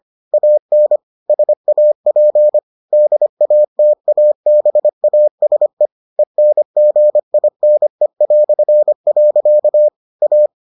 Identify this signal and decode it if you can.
Morse code, heard as a keyed tone, about 25 wpm, 600 Hz. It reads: AN SAP DATABASE ENGINE". A